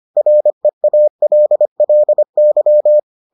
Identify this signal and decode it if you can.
Morse code, heard as a keyed tone, about 25 wpm, 600 Hz. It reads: REALLY